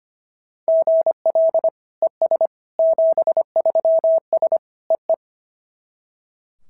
Morse code, heard as a keyed tone, about 22 wpm, 650 Hz. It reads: GL ES 73S EE
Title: SOTA QSO Practice 25wpm